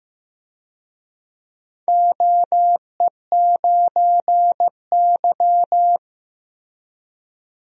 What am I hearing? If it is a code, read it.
Morse code, heard as a keyed tone, about 15 wpm, 700 Hz. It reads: OE9Y
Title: DXCC-100 Call Signs - 4 Characters - Repeated 2x - 15wpm